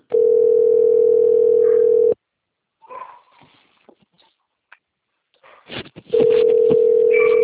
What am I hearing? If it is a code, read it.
Ringback tone (North America), 440 + 480 Hz.
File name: rg-1600-unknown- - - .3095